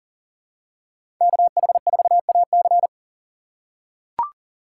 Morse code, heard as a keyed tone, about 40 wpm, 700 Hz. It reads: KH4AC
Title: USA Call Signs - 3x Character Spacing - 40wpm